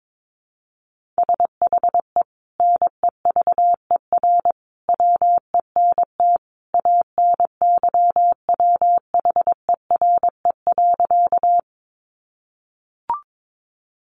Morse code, heard as a keyed tone, about 22 wpm, 700 Hz. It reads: SHE NEVER WENT ANYWHERE.
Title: Sentences from Top 1400 Words 22wpm